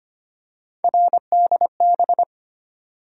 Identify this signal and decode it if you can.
Morse code, heard as a keyed tone, about 25 wpm, 700 Hz. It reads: RDB
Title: USA Call Sign Postfixes - 3 Letters 25wpm